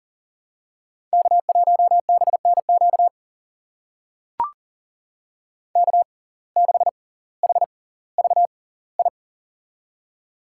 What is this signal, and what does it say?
Morse code, heard as a keyed tone, about 40 wpm, spaced as for 20 wpm, 700 Hz. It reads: K1BNQ K6HVI
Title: USA Call Signs - 5 Characters - 6x Character Spacing - 40wpm